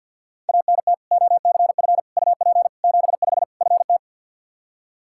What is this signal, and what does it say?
Morse code, heard as a keyed tone, about 53 wpm, 700 Hz. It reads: ANT OCF UP 75 FT
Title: Practice QSO - Exchange 2 - Info - 50wpm